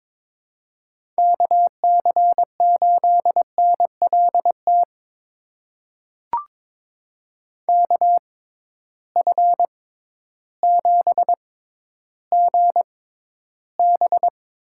Morse code, heard as a keyed tone, about 22 wpm, spaced as for 12 wpm, 700 Hz. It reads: KC8NLT KF7GB
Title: USA Call Signs - 6x Character Spacing - 22wpm